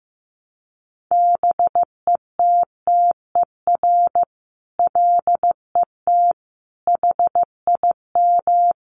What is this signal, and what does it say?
Morse code, heard as a keyed tone, about 15 wpm, 700 Hz. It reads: BETTER LET HIM
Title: Sentences from Top 1200 Words 15wpm